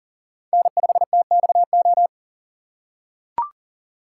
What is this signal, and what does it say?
Morse code, heard as a keyed tone, about 40 wpm, 700 Hz. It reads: N5TXO
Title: USA Call Signs - 5 Characters - 3x Character Spacing - 40wpm